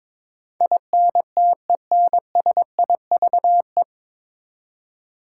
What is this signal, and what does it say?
Morse code, heard as a keyed tone, about 22 wpm, 700 Hz. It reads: INTENSIVE
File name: 9 Letter Words - First 3 Letters Given - 22wpm